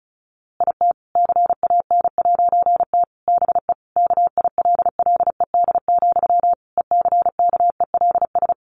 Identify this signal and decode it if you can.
Morse code, heard as a keyed tone, about 34 wpm, 700 Hz. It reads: IT CAN'T BE KILLED, ECKELS
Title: A Sound Of Thunder 35wpm